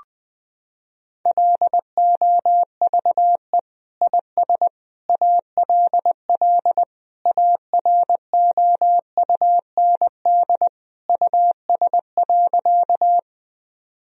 Morse code, heard as a keyed tone, about 21 wpm, 700 Hz. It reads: LOVE IS ALL AROUND US.
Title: Sentences from Top 1000 Words - Review Part 3 - 20wpm